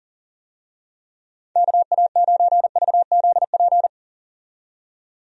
Morse code, heard as a keyed tone, about 40 wpm, 700 Hz. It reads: KA9VZP
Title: USA Call Signs - 6 Characters - 40wpm